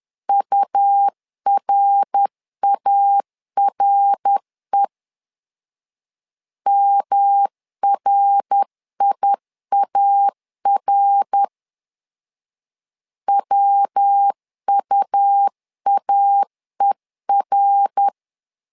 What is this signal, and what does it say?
Morse code, heard as a keyed tone, about 10 wpm, 800 Hz. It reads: URARE MRIAR WUAER
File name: LEZIONI / 3 MWRO / 3 MWR 10wpm 1952